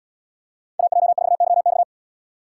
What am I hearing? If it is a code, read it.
Morse code, heard as a keyed tone, about 77 wpm, 700 Hz. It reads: AJ4YX